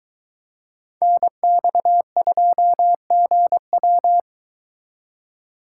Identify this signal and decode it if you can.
Morse code, heard as a keyed tone, about 22 wpm, 700 Hz. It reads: NX2GW